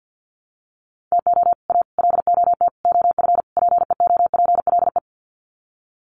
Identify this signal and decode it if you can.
Morse code, heard as a keyed tone, about 48 wpm, 700 Hz. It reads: TO A LOT OF PEOPLE
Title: Sets of 5 Words Encore 50wpm